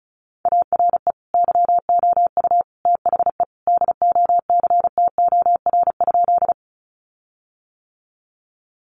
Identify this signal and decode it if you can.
Morse code, heard as a keyed tone, about 34 wpm, 700 Hz. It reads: ARE YOU THE DOCTOR?
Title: Sentences from Top 1500 Words 35wpm